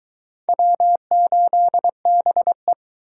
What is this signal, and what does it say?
Morse code, heard as a keyed tone, about 21 wpm, 700 Hz. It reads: W8BE